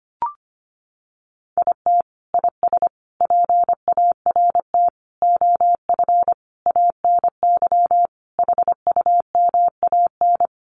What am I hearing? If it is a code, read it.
Morse code, heard as a keyed tone, about 25 wpm, 700 Hz. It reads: IT IS PART OF ANY HUMAN
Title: Sentences from Top 1100 Words 25wpm